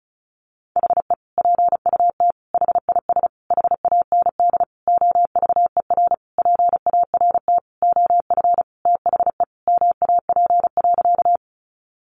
Morse code, heard as a keyed tone, about 34 wpm, 700 Hz. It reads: HE PUT HIS HAND OVER PART OF THE MAP.
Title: Sentences from Top 400 Words 35wpm